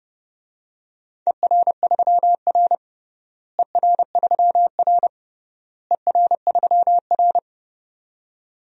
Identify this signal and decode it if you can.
Morse code, heard as a keyed tone, about 31 wpm, 700 Hz. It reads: ER3R ER3R ER3R
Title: DXCC-100 Call Signs - 4 Characters - Repeated 3x - 30wpm